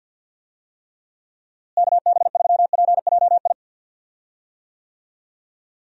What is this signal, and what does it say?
Morse code, heard as a keyed tone, about 46 wpm, 700 Hz. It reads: KB3PJI
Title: USA Call Signs - 6 Characters - Repeated 2x - 50wpm